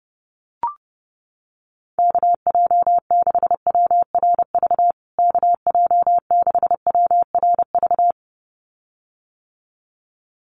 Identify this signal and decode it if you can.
Morse code, heard as a keyed tone, about 30 wpm, 700 Hz. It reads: KJ6WRV KJ6WRV
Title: USA Call Signs Repeated 2x 30wpm